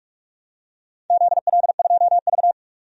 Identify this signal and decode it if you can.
Morse code, heard as a keyed tone, about 46 wpm, 700 Hz. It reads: ZL2V
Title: DXCC-100 Call Signs - 4 Characters - 45wpm